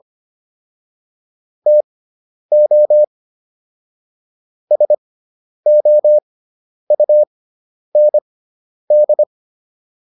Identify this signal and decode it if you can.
Morse code, heard as a keyed tone, about 25 wpm, spaced as for 10 wpm, 600 Hz. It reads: TO SOUND